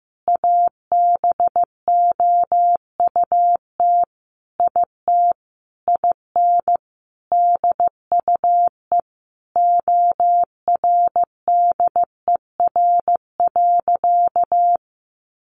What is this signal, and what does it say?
Morse code, heard as a keyed tone, about 15 wpm, 700 Hz. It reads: ABOUT IT IN DUE ORDER.
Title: Sentences from Top 900 Words 15wpm